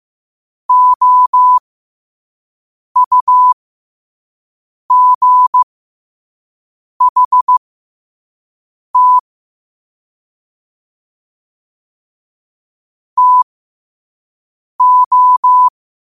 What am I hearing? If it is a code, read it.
Morse code, heard as a keyed tone, about 15 wpm, spaced as for 5 wpm, 1000 Hz. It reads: OUGHT TO
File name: qotdfiles / 5wpm